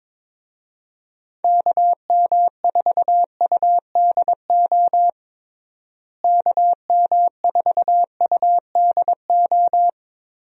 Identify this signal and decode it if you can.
Morse code, heard as a keyed tone, about 22 wpm, 700 Hz. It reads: KM4UDO KM4UDO
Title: USA Call Signs - 6 Characters - Repeated 2x - 22wpm